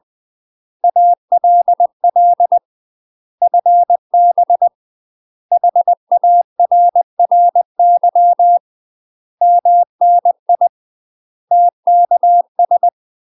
Morse code, heard as a keyed tone, about 20 wpm, 700 Hz. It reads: ALL FB HARRY MNI TKS
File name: UK MASTER QSOs TRAINING / 20WPM / UK MASTER QSOs 0053